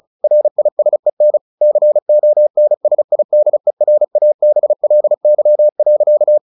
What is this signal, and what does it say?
Morse code, heard as a keyed tone, about 34 wpm, 600 Hz. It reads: RISEN CONSIDERABLY.
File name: the old man and the sea 2 35wpm